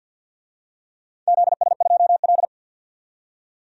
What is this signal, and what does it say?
Morse code, heard as a keyed tone, about 53 wpm, 700 Hz. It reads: ZS2L
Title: DXCC-100 Call Signs - 4 Characters - 50wpm